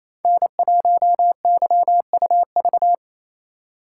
Morse code, heard as a keyed tone, about 28 wpm, 700 Hz. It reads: N1YUV
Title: USA Call Signs - 4x Character Spacing - 28wpm